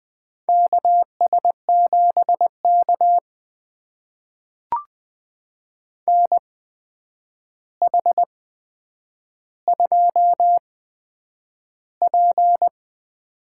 Morse code, heard as a keyed tone, about 20 wpm, spaced as for 9 wpm, 700 Hz. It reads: KS7K NH2P
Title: USA Call Signs - 8x Character Spacing - 20wpm